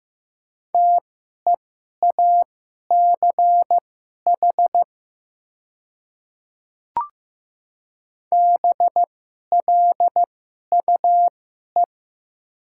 Morse code, heard as a keyed tone, about 15 wpm, spaced as for 11 wpm, 700 Hz. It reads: TEACH BLUE
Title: 2x Character Spacing - Top 500 Words - 15wpm